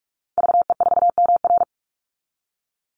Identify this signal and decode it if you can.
Morse code, heard as a keyed tone, about 44 wpm, 700 Hz. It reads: VE4MR
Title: Call Signs Repeated 3x - 45wpm